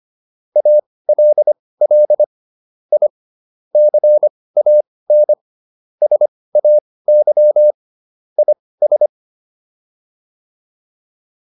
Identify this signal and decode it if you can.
Morse code, heard as a keyed tone, about 25 wpm, spaced as for 18 wpm, 600 Hz. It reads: ALL I CAN SAY IS